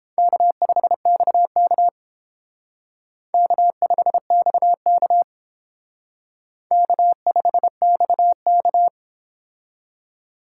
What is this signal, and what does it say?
Morse code, heard as a keyed tone, about 30 wpm, 700 Hz. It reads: K5XK K5XK K5XK